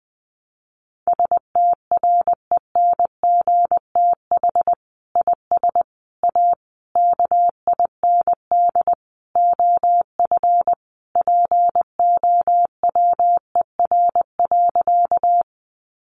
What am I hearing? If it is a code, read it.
Morse code, heard as a keyed tone, about 20 wpm, 700 Hz. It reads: STRENGTH IS A KIND OF POWER.
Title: Sentences from Top 1300 Words 20wpm